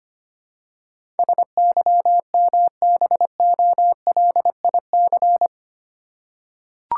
Morse code, heard as a keyed tone, about 25 wpm, 700 Hz. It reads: SYMBOLIC
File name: 8 Letter Words - First 3 Letters Given - 25wpm